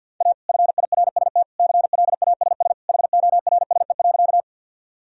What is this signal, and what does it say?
Morse code, heard as a keyed tone, about 53 wpm, 700 Hz. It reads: A FIRST CLASS HORSE.
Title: Sentences from Top 1000 Words - Review Part 2 - 50wpm